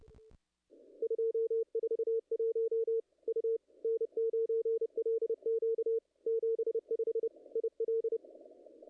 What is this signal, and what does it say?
Morse code, heard as a keyed tone, about 30 wpm, 440 Hz. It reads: 241 U N9LQ 75 IL
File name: contests / ss-cw-k5tr-op-n5rz -0100